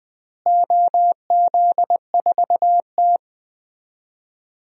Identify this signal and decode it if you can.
Morse code, heard as a keyed tone, about 20 wpm, 700 Hz. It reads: OZ4T
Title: DXCC-100 Call Signs - 4 Characters - Repeated 2x - 20wpm